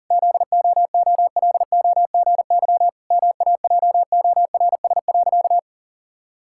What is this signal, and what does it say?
Morse code, heard as a keyed tone, about 40 wpm, 700 Hz. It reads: ZOOLOGY MAJORS.
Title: Pangram Challenge v2 - 40wpm